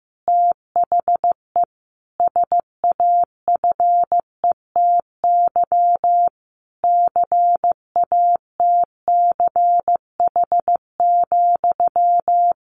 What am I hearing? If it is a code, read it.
Morse code, heard as a keyed tone, about 15 wpm, 700 Hz. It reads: THE SAFETY CATCH,
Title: A Sound Of Thunder 15wpm